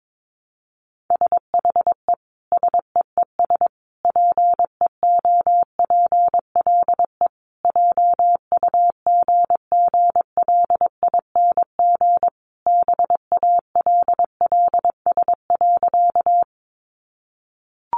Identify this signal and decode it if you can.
Morse code, heard as a keyed tone, about 22 wpm, 700 Hz. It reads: SHE SEES PEOPLE JUGGLING BALLS.
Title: Easy Phrases 22wpm